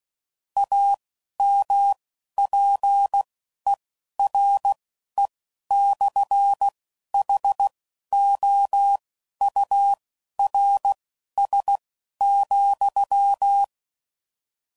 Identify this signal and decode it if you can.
Morse code, heard as a keyed tone, about 16 wpm, spaced as for 11 wpm, 800 Hz. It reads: AMPERE/HOURS,
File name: CW Code Groups @10WPM / Technical Subjects / SwPwrSup